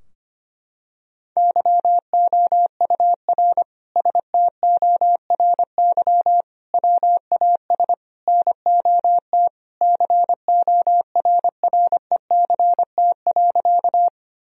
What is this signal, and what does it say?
Morse code, heard as a keyed tone, about 25 wpm, 700 Hz. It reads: YOUR STORY WAS NOT CORRECT.
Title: Sentences from Top 1000 Words - Review Part 2 - 25wpm